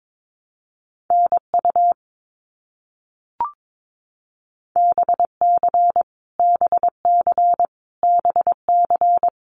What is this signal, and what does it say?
Morse code, heard as a keyed tone, about 22 wpm, 700 Hz. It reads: NU BC BC BC
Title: Canadian Provinces and Territories - Repeated 3x - 22wpm